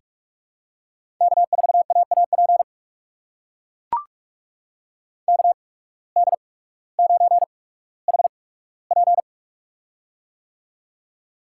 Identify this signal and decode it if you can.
Morse code, heard as a keyed tone, about 44 wpm, spaced as for 19 wpm, 700 Hz. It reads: K4AAP KD9HP
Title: USA Call Signs - 5 Characters - 8x Character Spacing - 45wpm